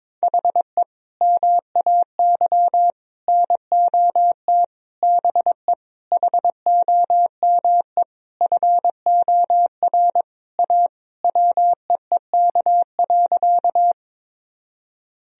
Morse code, heard as a keyed tone, about 22 wpm, 700 Hz. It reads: HE MAY NOT BE HOME FOR A WEEK.
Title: Sentences from Top 1000 Words - Review Part 2 - 22wpm